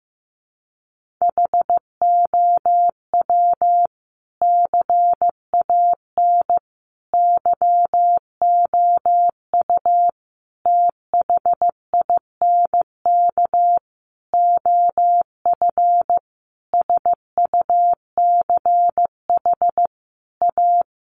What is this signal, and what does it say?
Morse code, heard as a keyed tone, about 15 wpm, 700 Hz. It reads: HOW CAN YOU THINK OF SUCH A
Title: Sentences from Top 200 Words 15wpm